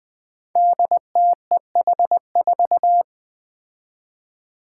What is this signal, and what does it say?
Morse code, heard as a keyed tone, about 20 wpm, 700 Hz. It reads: DTEH4
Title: Five Challenging Letter-Number Combinations 20wpm